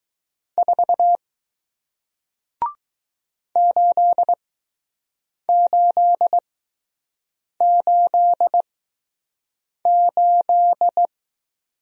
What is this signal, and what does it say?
Morse code, heard as a keyed tone, about 20 wpm, spaced as for 6 wpm, 700 Hz. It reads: 4 8888